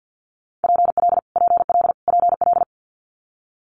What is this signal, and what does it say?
Morse code, heard as a keyed tone, about 50 wpm, 700 Hz. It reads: PL PL PL
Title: Multi-letter Phonemes Repeated 3x 50wpm